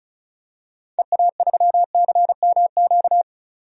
Morse code, heard as a keyed tone, about 34 wpm, 700 Hz. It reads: EA3CMQ
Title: DXCC-100 Call Signs - 6 Characters - 35wpm